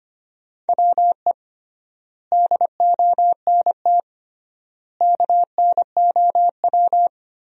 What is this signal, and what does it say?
Morse code, heard as a keyed tone, about 25 wpm, 700 Hz. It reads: WE DONT KNOW
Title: 3x Word-Spacing - Sentences from Top 400 Words - 25wpm